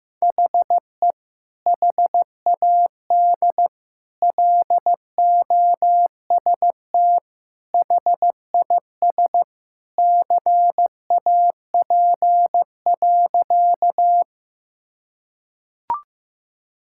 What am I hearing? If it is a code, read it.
Morse code, heard as a keyed tone, about 16 wpm, 700 Hz. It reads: HE HAD LOST HIS CAP.
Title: Sentences from Top 1600 Words 15wpm